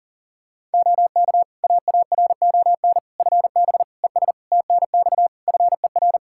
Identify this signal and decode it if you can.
Morse code, heard as a keyed tone, about 40 wpm, 700 Hz. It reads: OK AARON FB ES TNX FER
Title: K1USN SST QSO Practice 40wpm